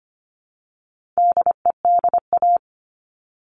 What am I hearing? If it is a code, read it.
Morse code, heard as a keyed tone, about 25 wpm, 700 Hz. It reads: DEDA
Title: All Cut Numbers - 4-Digit 25wpm